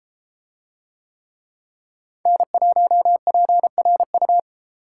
Morse code, heard as a keyed tone, about 32 wpm, 700 Hz. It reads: N1PRU